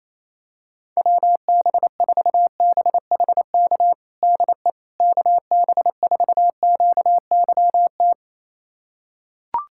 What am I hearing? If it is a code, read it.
Morse code, heard as a keyed tone, about 28 wpm, 700 Hz. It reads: WB4BHK DE KB4QYT
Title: USA Callsign DE Callsign - 6 Characters - 28wpm